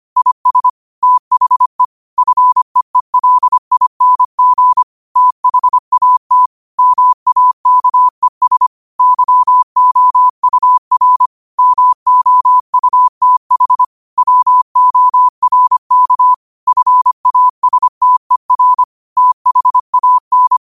Morse code, heard as a keyed tone, about 25 wpm, 1000 Hz. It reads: IS THE FEELING THAT MAKES YOUR MOUTH WORK FASTER THAN
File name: qotdfiles / 25wpm